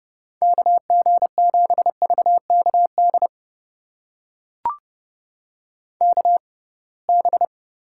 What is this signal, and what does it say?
Morse code, heard as a keyed tone, about 30 wpm, spaced as for 23 wpm, 700 Hz. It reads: KG7VKD KB
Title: USA Call Signs - 6 Characters - 6x Character Spacing - 30wpm